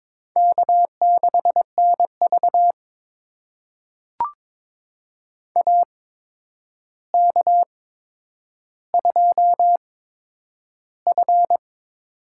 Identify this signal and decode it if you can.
Morse code, heard as a keyed tone, about 22 wpm, spaced as for 9 wpm, 700 Hz. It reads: K6NV AK2F
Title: USA Call Signs - 4 Characters - 8x Character Spacing - 22wpm